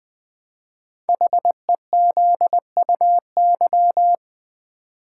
Morse code, heard as a keyed tone, about 20 wpm, 700 Hz. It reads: HEZUY
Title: Code Groups - 5 Letters - 20wpm